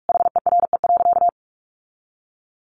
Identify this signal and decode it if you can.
Morse code, heard as a keyed tone, about 44 wpm, 700 Hz. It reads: HERE.
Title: Sentences from Top 1200 Words 45wpm